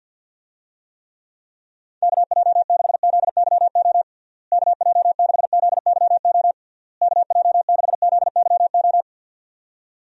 Morse code, heard as a keyed tone, about 50 wpm, 700 Hz. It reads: KJ6ZYO KJ6ZYO KJ6ZYO
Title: USA Call Signs - 6 Characters - Repeated 3x - 50wpm